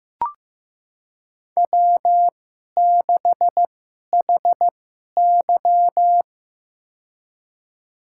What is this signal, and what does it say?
Morse code, heard as a keyed tone, about 15 wpm, spaced as for 11 wpm, 700 Hz. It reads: W6HY
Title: USA Call Signs - 4 Characters - 2x Character Spacing - 15wpm